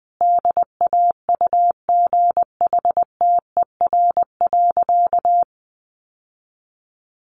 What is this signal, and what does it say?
Morse code, heard as a keyed tone, about 19 wpm, 700 Hz. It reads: DAUGHTER.
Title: Sentences from Top 1500 Words 20wpm